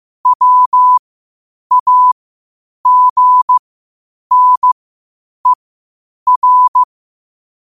Morse code, heard as a keyed tone, about 15 wpm, spaced as for 8 wpm, 1000 Hz. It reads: WAGNER
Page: Quotes for Sat, 16 Aug 2025 in Morse Code at 8 words per minute.